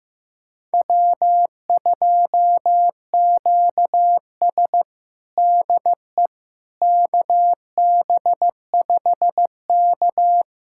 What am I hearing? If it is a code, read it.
Morse code, heard as a keyed tone, about 15 wpm, 700 Hz. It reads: W2QS DE KB5K
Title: USA Callsign DE Callsign - 4 Characters - 15wpm